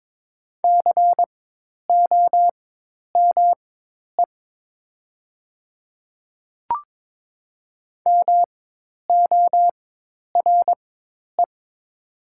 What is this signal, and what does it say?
Morse code, heard as a keyed tone, about 22 wpm, spaced as for 10 wpm, 700 Hz. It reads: COME MORE
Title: 4x Character Spacing - Top 100 Words - 22wpm